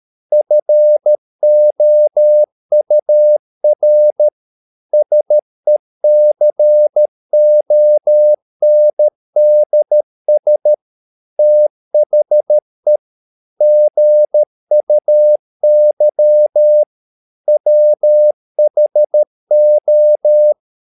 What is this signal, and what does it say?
Morse code, heard as a keyed tone, about 13 wpm, 600 Hz. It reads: FOUR SECONDS THE GUY WHO